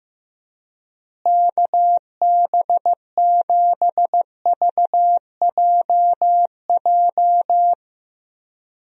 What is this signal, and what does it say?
Morse code, heard as a keyed tone, about 15 wpm, 700 Hz. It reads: KB7VJJ
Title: USA Call Signs - 6 Characters - Repeated 3x - 15wpm